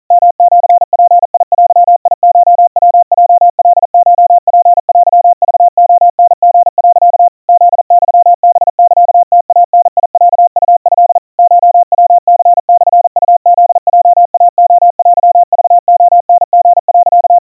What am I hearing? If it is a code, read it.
Morse code (Japanese Wabun code), heard as a keyed tone, about 40 wpm, 700 Hz. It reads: ヨシヅデコヤヲカコツテクレタリ、 フユハサムイダロウト コヤニモウフヲイレテクレタリ、
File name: kawa / Morse sample / kamyu 150 cpm 700 Hz Crispy